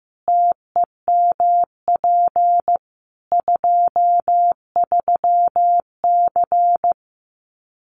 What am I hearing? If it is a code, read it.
Morse code, heard as a keyed tone, about 15 wpm, 700 Hz. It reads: TEMP 23C
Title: Common QSO Phrases 15wpm